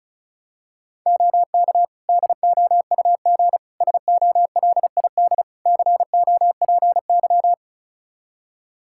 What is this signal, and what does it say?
Morse code, heard as a keyed tone, about 36 wpm, 700 Hz. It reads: OK DOUG SOLID COPY
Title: K1USN SST QSO Practice 35wpm